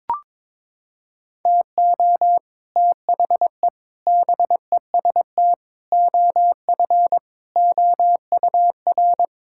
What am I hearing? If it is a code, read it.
Morse code, heard as a keyed tone, about 22 wpm, 700 Hz. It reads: TO THE BEST OF OUR
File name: Sentences with 9-Word Phrases - 22wpm